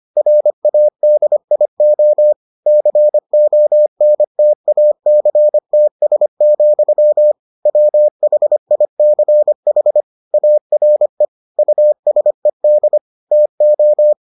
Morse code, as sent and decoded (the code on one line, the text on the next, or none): .-. .- -.. .. --- / -.-. --- -. - .- -.-. - ... --..-- / .-- .... .. -.-. .... / .- .-. / ..- ... -.. / - ---
RADIO CONTACTS, WHICH ARE USED TO